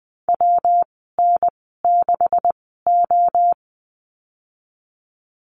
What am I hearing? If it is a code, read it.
Morse code, heard as a keyed tone, about 19 wpm, spaced as for 14 wpm, 700 Hz. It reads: WN6O
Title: 128 Lesson 21g - Number 6 - Call Signs 20wpm